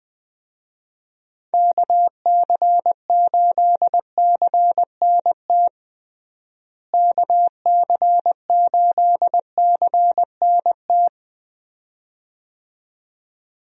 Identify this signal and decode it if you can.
Morse code, heard as a keyed tone, about 20 wpm, 700 Hz. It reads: KC8CNT KC8CNT
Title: USA Call Signs - 6 Characters - Repeated 2x - 20wpm